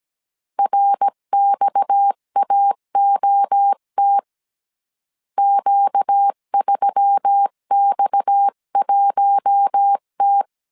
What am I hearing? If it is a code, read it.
Morse code, heard as a keyed tone, about 17 wpm, 800 Hz. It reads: RXAOT Q3X1T
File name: CW / DET / 1 6 1500C S C 16wpm 2733